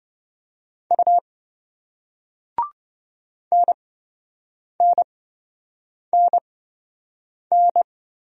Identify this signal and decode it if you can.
Morse code, heard as a keyed tone, about 27 wpm, spaced as for 7 wpm, 700 Hz. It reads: U NNNN